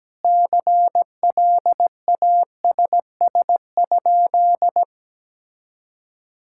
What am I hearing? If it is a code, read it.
Morse code, heard as a keyed tone, about 17 wpm, 700 Hz. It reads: CLASS?
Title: Sentences from Top 1000 Words - Review Part 6 - 17wpm